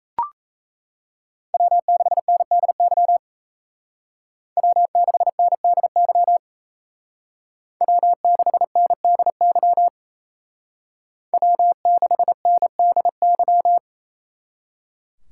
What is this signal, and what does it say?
Morse code, heard as a keyed tone, about 36 wpm, 700 Hz. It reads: W6NDY W6NDY W6NDY W6NDY